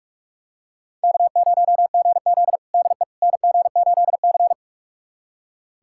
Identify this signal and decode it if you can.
Morse code, heard as a keyed tone, about 41 wpm, 700 Hz. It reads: K0GZ DE NG8C
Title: USA Callsign DE Callsign - 4 Characters - 45wpm